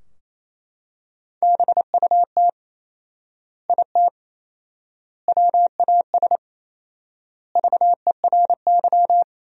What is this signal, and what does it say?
Morse code, heard as a keyed tone, about 28 wpm, 700 Hz. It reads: BUT IT WAS VERY
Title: 4x Word-Spacing - Sentences from Top 300 Words - 28wpm